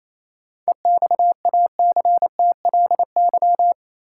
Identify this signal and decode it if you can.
Morse code, heard as a keyed tone, about 28 wpm, 700 Hz. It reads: EXACTLY
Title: Sentences from Top 900 Words - New Word Drill - 28wpm